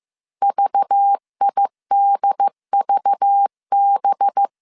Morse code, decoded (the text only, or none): VIDVB